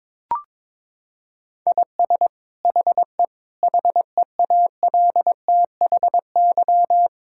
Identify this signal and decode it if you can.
Morse code, heard as a keyed tone, about 22 wpm, 700 Hz. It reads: IS HE HEALTHY
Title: Sentences from Top 1000 Words - Review Part 5 - 22wpm